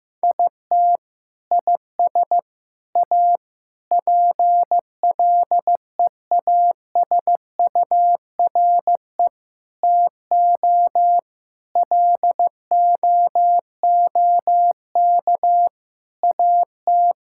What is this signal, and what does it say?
Morse code, heard as a keyed tone, about 15 wpm, 700 Hz. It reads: IT IS A PLEASURE TO LOOK AT
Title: Sentences from Top 1600 Words 15wpm